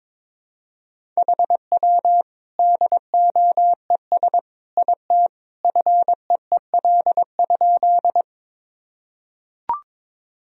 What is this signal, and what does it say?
Morse code, heard as a keyed tone, about 22 wpm, 700 Hz. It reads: HW DOES IT FEEL?
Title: Sentences from Top 400 Words and Abbreviations 22wpm